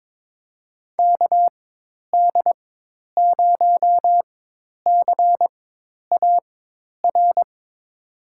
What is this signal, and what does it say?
Morse code, heard as a keyed tone, about 22 wpm, spaced as for 10 wpm, 700 Hz. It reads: KD0CAR
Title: USA Call Signs - 6 Characters - 4x Character Spacing - 22wpm